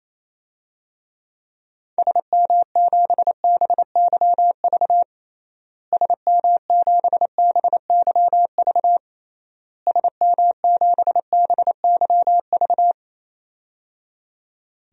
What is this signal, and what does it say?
Morse code, heard as a keyed tone, about 28 wpm, 700 Hz. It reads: SM7BYV SM7BYV SM7BYV
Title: DXCC-100 Call Signs - 6 Characters - Repeated 3x - 28wpm